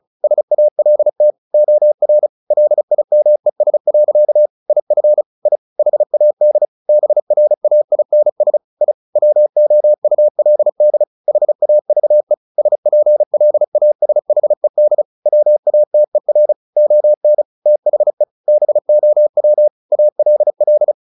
Morse code, heard as a keyed tone, about 35 wpm, 600 Hz. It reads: SALT OR LIMES. IF I HAD BRAINS I WOULD HAVE SPLASHED WATER ON THE BOW ALL
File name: the old man and the sea 2 35wpm